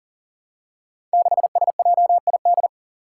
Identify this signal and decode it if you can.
Morse code, heard as a keyed tone, about 39 wpm, 700 Hz. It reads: 6SJID
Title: Five Challenging Letter-Number Combinations 40wpm